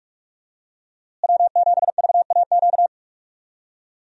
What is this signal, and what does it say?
Morse code, heard as a keyed tone, about 47 wpm, 700 Hz. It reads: W7VAQ
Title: USA Call Signs - 5 Characters - Repeated 2x - 45wpm